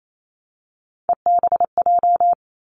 Morse code, heard as a keyed tone, about 28 wpm, 700 Hz. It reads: EBJ
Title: Three Challenging Letter-Number Combinations 28wpm